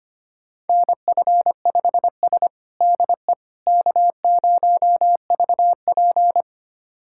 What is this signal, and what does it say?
Morse code, heard as a keyed tone, about 25 wpm, 700 Hz. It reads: NF5S DE K0VP
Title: USA Callsign DE Callsign - 4 Characters - 25wpm